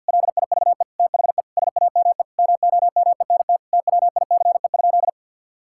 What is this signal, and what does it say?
Morse code, heard as a keyed tone, about 50 wpm, 700 Hz. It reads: LIVE THE SAME MOMENT TWICE?
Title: Sentences from Top 1600 Words 50wpm